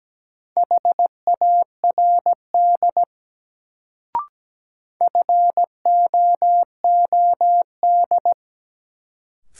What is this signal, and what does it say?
Morse code, heard as a keyed tone, about 17 wpm, 700 Hz. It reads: HARD FOOD
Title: Sentences from Top 300 Words - New Word Drill - 17wpm